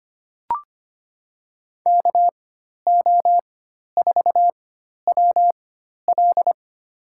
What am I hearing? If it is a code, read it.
Morse code, heard as a keyed tone, about 25 wpm, spaced as for 12 wpm, 700 Hz. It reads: KO4WL
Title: USA Call Signs - 5 Characters - 4x Character Spacing - 25wpm